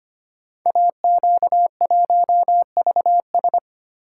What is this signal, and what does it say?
Morse code, heard as a keyed tone, about 25 wpm, 700 Hz. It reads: AQ1VS
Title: Code Groups - 5 Letter:Numbers - 25wpm